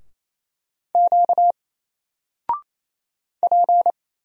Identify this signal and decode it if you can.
Morse code, heard as a keyed tone, about 27 wpm, 700 Hz. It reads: Q P
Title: Single Character (Letter:Number:Punctuation:Prosign) 28wpm